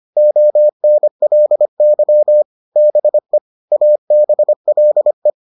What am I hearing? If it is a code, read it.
Morse code, heard as a keyed tone, about 25 wpm, 600 Hz. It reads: ONLY BE ABLE